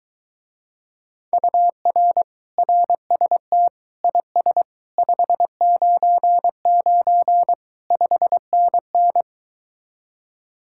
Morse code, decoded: UR RST IS 599 5NN

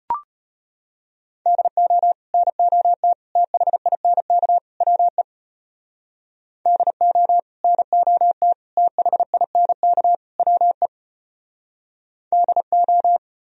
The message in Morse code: DO NOT THINK WE DO NOT THINK WE DO

